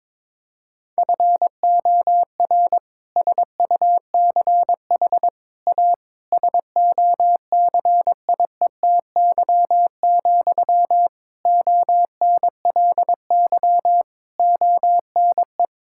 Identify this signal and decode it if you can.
Morse code, heard as a keyed tone, about 22 wpm, 700 Hz. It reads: FOR SUCH A SOCIETY, ONLY ONE
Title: Sentences from Top 1000 Words - Review Part 6 - 22wpm